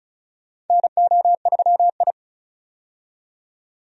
Morse code, heard as a keyed tone, about 36 wpm, 700 Hz. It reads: NO3I
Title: USA Call Signs - 4 Characters - 35wpm